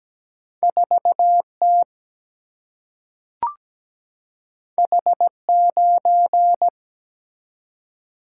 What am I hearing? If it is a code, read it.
Morse code, heard as a keyed tone, about 17 wpm, 700 Hz. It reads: 4 T H 9
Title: Single-Letter Single-Number 17wpm